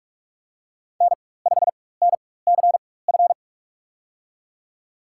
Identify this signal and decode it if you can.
Morse code, heard as a keyed tone, about 41 wpm, spaced as for 20 wpm, 700 Hz. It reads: N5NCF
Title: USA Call Signs - 5 Characters - 4x Character Spacing - 45wpm